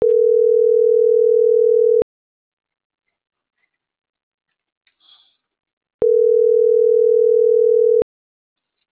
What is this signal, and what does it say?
Ringback tone (North America), 440 + 480 Hz.